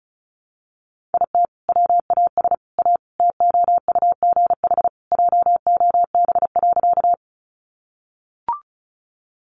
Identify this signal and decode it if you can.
Morse code, heard as a keyed tone, about 34 wpm, 700 Hz. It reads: IT WAS A TOUGH JOB.
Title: Sentences from Top 1200 Words 35wpm